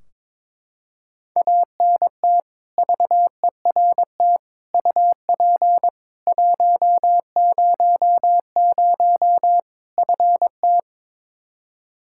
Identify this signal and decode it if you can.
Morse code, heard as a keyed tone, about 22 wpm, 700 Hz. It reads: ANT VERT UP 100 FT
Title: Practice QSO - Exchange 2 - Info - 22wpm